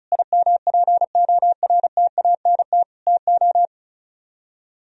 Morse code, heard as a keyed tone, about 45 wpm, 700 Hz. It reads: IMPORTANT TO